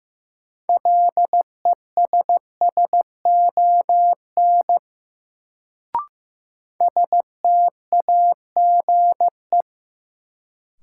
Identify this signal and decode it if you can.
Morse code, heard as a keyed tone, about 15 wpm, 700 Hz. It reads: LESSON STAGE
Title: Sentences from Top 1200 Words - New Word Drill - 15wpm